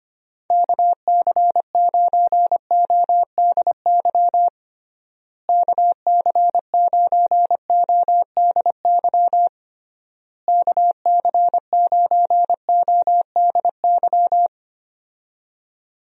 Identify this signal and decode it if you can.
Morse code, heard as a keyed tone, about 25 wpm, 700 Hz. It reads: KC9ODY KC9ODY KC9ODY
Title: USA Call Signs - 6 Characters - Repeated 3x - 25wpm